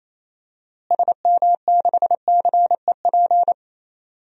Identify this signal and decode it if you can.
Morse code, heard as a keyed tone, about 28 wpm, 700 Hz. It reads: SM6CEP
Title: DXCC-100 Call Signs - 6 Characters - Repeated 3x - 28wpm